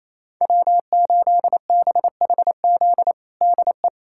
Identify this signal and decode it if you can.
Morse code, heard as a keyed tone, about 28 wpm, 700 Hz. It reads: W8BHZ DE
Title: USA Callsign DE Callsign - 5 Characters - 28wpm